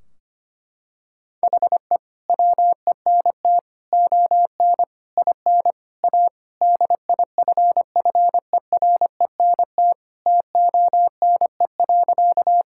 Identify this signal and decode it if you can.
Morse code, heard as a keyed tone, about 25 wpm, 700 Hz. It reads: HE WENT ON IN A DIFFERENT TONE.
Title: Sentences from Top 1600 Words 25wpm